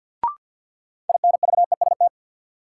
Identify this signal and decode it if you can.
Morse code, heard as a keyed tone, about 52 wpm, 700 Hz. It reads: INVEST
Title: Sentences from Top 1100 Words - New Word Drill Alt - 50wpm